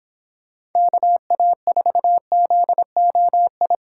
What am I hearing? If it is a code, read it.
Morse code, heard as a keyed tone, about 28 wpm, 700 Hz. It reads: KA4ZOI